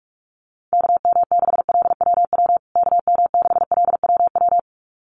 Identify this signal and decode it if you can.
Morse code, heard as a keyed tone, about 44 wpm, 700 Hz. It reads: KM6LWW KM6LWW
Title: USA Call Signs Repeated 2x 45wpm